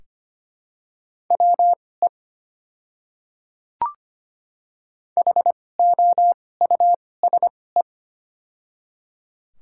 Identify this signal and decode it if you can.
Morse code, heard as a keyed tone, about 25 wpm, spaced as for 18 wpm, 700 Hz. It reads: WE HOUSE
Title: 2x Character Spacing - Top 300 Words - 25wpm